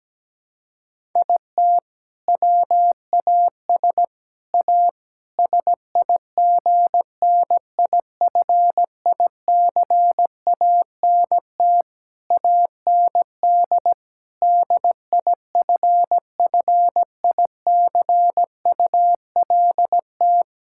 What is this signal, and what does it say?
Morse code, heard as a keyed tone, about 17 wpm, 700 Hz. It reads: IT WAS A SIGNIFICANT AND DIFFICULT
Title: Sentences from Top 1000 Words - Review Part 6 - 17wpm